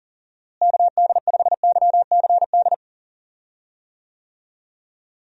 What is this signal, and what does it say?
Morse code, heard as a keyed tone, about 37 wpm, 700 Hz. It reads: KD5YCD
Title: USA Call Signs - 6 Characters - Repeated 2x - 40wpm